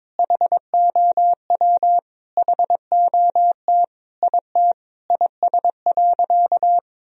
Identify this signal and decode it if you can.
Morse code, heard as a keyed tone, about 22 wpm, 700 Hz. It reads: HOW HOT IT IS.
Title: Sentences from Top 1000 Words - Review Part 3 - 22wpm